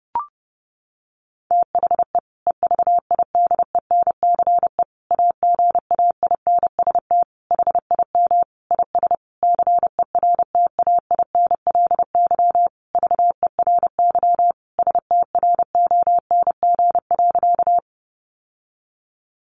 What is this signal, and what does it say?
Morse code, heard as a keyed tone, about 30 wpm, 700 Hz. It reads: THE EVIDENCE AGAINST HIM IS CERTAINLY VERY STRONG.
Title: Sentences from Top 1300 Words 30wpm